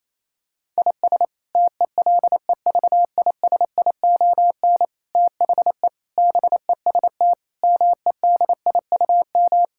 Morse code, heard as a keyed tone, about 28 wpm, 700 Hz. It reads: IS TELEVISION THE BEST MEDIUM
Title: Sentences from Top 1000 Words - Review Part 6 - 28wpm